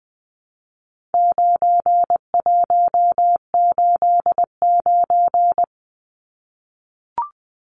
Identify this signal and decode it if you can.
Morse code, heard as a keyed tone, about 20 wpm, 700 Hz. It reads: 9189
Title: Numbers - 4-Digit 20wpm